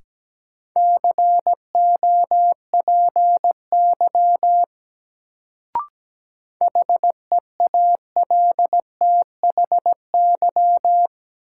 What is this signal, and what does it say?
Morse code, heard as a keyed tone, about 17 wpm, 700 Hz. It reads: COPY HEALTHY
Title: Sentences from Top 1400 Words - New Word Drill Alt - 17wpm